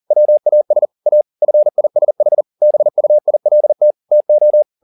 Morse code, heard as a keyed tone, about 40 wpm, 600 Hz. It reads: WAS A FISH BUILT TO
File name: the old man and the sea 2 40wpm